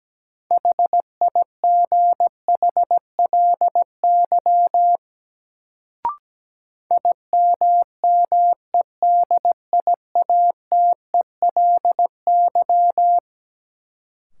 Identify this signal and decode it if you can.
Morse code, heard as a keyed tone, about 17 wpm, 700 Hz. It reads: HIGHLY IMMEDIATELY
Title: Sentences from Top 1100 Words - New Word Drill Alt - 17wpm